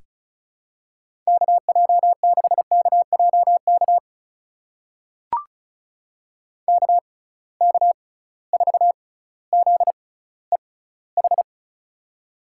Morse code, heard as a keyed tone, about 36 wpm, spaced as for 18 wpm, 700 Hz. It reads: KJ6KJK KK4ZEH
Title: USA Call Signs - 6 Characters - 6x Character Spacing - 35wpm